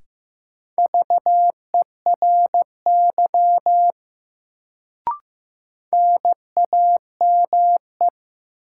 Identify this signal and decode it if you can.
Morse code, heard as a keyed tone, about 15 wpm, 700 Hz. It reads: VERY NAME
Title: Sentences from Top 200 Words - New Word Drill - 15wpm